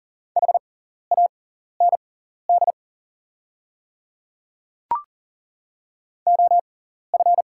Morse code, heard as a keyed tone, about 41 wpm, spaced as for 14 wpm, 690 Hz. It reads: HAND OF
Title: 6x Character Spacing - Top 200 Words - 40wpm